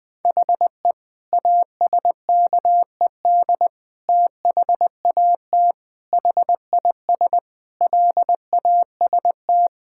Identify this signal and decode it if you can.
Morse code, heard as a keyed tone, about 20 wpm, 700 Hz. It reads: HE ASKED THAT HIS LAST
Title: Sentences with 9-Word Phrases - 20wpm